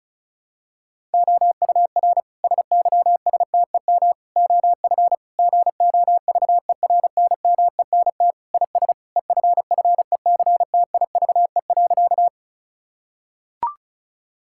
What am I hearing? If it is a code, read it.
Morse code, heard as a keyed tone, about 36 wpm, 700 Hz. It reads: OUR SYSTEM OF GOVERNMENT IS EFFECTIVE.
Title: Sentences from Top 1000 Words - Review Part 6 - 35wpm